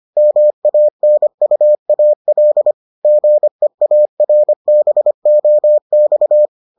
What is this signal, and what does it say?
Morse code, heard as a keyed tone, about 25 wpm, 600 Hz. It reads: MANUAL GEARBOX